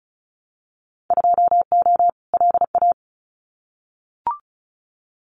Morse code, heard as a keyed tone, about 34 wpm, 700 Hz. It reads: 2O LA
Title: Winter Field Day Exchanges 35wpm